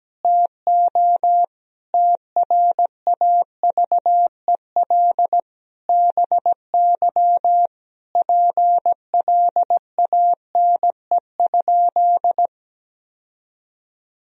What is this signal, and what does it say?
Morse code, heard as a keyed tone, about 17 wpm, 700 Hz. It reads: TO TRAVEL BY PLANE?
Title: Sentences from Top 1000 Words - Review Part 3 - 17wpm